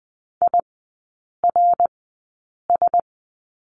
Morse code, heard as a keyed tone, about 20 wpm, spaced as for 8 wpm, 700 Hz. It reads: IRS
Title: 053 Lesson 10e - Letter r - Three Characters 20wpm